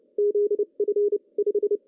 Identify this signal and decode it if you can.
Morse code, heard as a keyed tone, about 26 wpm, 380 Hz. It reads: ZF 5
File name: CW / cqww-cw-k5tr